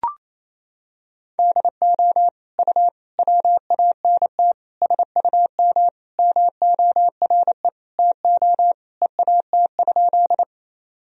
Morse code, heard as a keyed tone, about 27 wpm, 700 Hz. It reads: DO U WANT SUM MORE TO EAT?
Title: Sentences from Top 300 Words and Abbreviations 28wpm